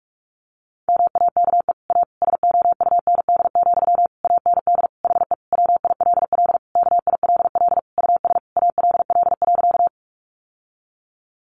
Morse code, heard as a keyed tone, about 44 wpm, 700 Hz. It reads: MAKE A SOUND, AND HE WILL KILL US ALL.
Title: Sentences from Top 1400 Words 45wpm